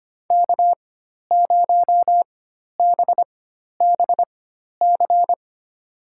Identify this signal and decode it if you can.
Morse code, heard as a keyed tone, about 25 wpm, spaced as for 12 wpm, 700 Hz. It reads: K0BBC
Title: USA Call Signs - 5 Characters - 4x Character Spacing - 25wpm